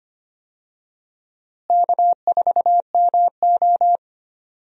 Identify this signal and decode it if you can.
Morse code, heard as a keyed tone, about 25 wpm, 700 Hz. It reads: K4MO
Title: USA Call Signs - 4 Characters - Repeated 3x - 25wpm